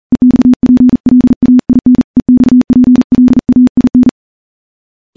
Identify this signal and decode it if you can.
Morse code, heard as a keyed tone, about 40 wpm, 250 Hz. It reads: EXPLAIN EXPLAIN
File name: morse-training-40WPM